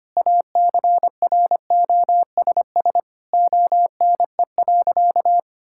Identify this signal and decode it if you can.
Morse code, heard as a keyed tone, about 25 wpm, 700 Hz. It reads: ACROSS ONE.
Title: Sentences from Top 1000 Words - Review Part 5 - 25wpm